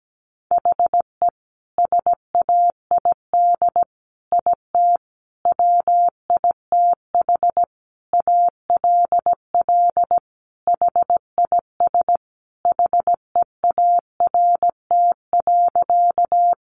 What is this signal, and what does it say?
Morse code, heard as a keyed tone, about 17 wpm, 700 Hz. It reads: HE SAID IT WITH ALL HIS HEART.
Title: Sentences from Top 1200 Words 17wpm